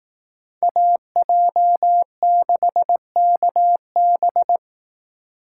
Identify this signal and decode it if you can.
Morse code, heard as a keyed tone, about 19 wpm, 700 Hz. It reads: AJ6KB
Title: Speed Racing - USA 2x2 Call Signs 15wpm